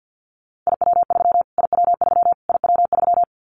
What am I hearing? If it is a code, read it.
Morse code, heard as a keyed tone, about 50 wpm, 700 Hz. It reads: IW3 IW3 IW3
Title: Callsign Prefixes for Top 25 DXCC - Repeated 3x - 50wpm